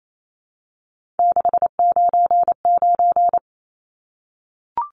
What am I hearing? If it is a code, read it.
Morse code, heard as a keyed tone, about 28 wpm, 700 Hz. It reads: 699
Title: Numbers - 3-Digit 28wpm